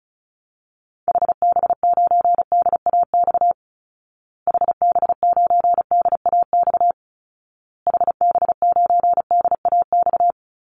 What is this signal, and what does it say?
Morse code, heard as a keyed tone, about 34 wpm, 700 Hz. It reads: HB9DAX HB9DAX HB9DAX
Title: Call Signs Repeated 2x - 35wpm